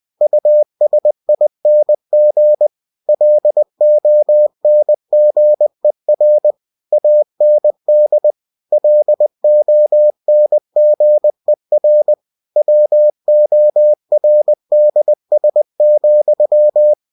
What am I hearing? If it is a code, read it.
Morse code, heard as a keyed tone, about 20 wpm, 600 Hz. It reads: USING LONGER AND LONGER WORDS,
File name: winnie-20wpm